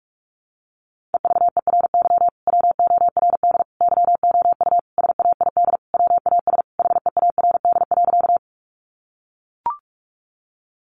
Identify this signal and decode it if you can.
Morse code, heard as a keyed tone, about 44 wpm, 700 Hz. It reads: EVERY WORD YOU SAID WAS HEARD.
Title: Sentences from Top 400 Words 45wpm